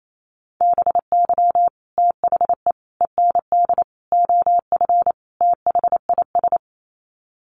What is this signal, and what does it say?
Morse code, heard as a keyed tone, about 28 wpm, 700 Hz. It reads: BY THE END OF THIS
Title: Sets of 5 Words Encore 28wpm